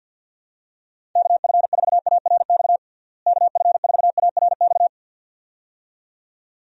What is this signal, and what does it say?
Morse code, heard as a keyed tone, about 52 wpm, 700 Hz. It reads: KF4ARX KF4ARX
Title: USA Call Signs - 6 Characters - Repeated 2x - 50wpm